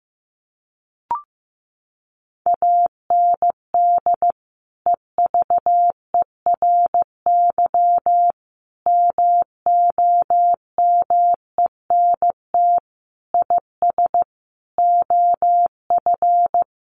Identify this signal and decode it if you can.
Morse code, heard as a keyed tone, about 15 wpm, 700 Hz. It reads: AND EVERY MOMENT IS OF
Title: Sentences from Top 1200 Words 15wpm